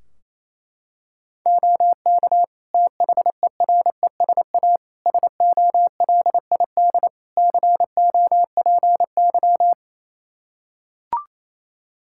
Morse code, heard as a keyed tone, about 28 wpm, 700 Hz. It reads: OK THERESA SOLID COPY
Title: Practice QSO - Exchange 3 - More Info - 28wpm